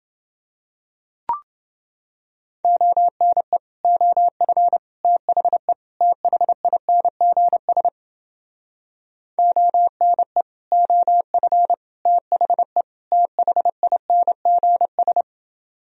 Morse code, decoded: ONE OF THE THINGS ONE OF THE THINGS

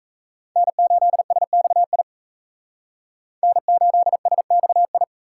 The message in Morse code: N8SXI N8SXI

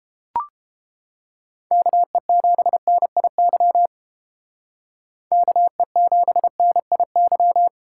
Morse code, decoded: KE7NIY KE7NIY

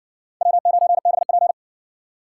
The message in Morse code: W9BP

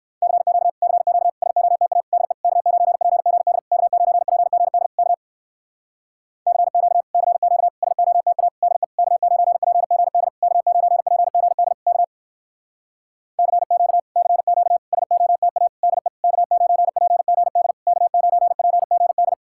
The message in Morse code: CQ CQ SOTA DE K9PGD K9PGD K CQ CQ SOTA DE K9PGD K9PGD K CQ CQ SOTA DE K9PGD K9PGD

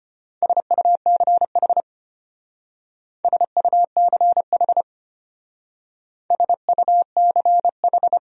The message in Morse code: SUCH SUCH SUCH

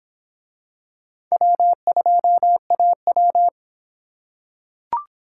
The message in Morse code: W2AW

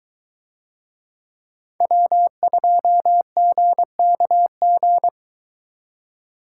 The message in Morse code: W2GKG

